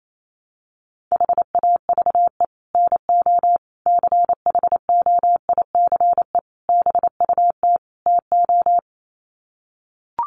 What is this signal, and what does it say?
Morse code, heard as a keyed tone, about 28 wpm, 700 Hz. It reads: HAVE NO CHOICE BUT TO
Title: Sets of 5 Words Encore 28wpm